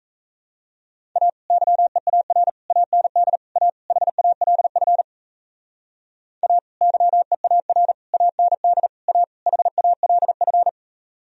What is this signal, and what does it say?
Morse code, heard as a keyed tone, about 36 wpm, 700 Hz. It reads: A YEAR AND A HALF A YEAR AND A HALF